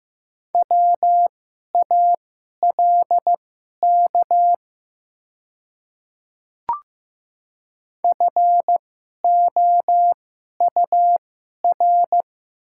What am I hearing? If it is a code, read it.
Morse code, heard as a keyed tone, about 15 wpm, spaced as for 11 wpm, 700 Hz. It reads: WALK FOUR
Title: 2x Character Spacing - Top 300 Words - 15wpm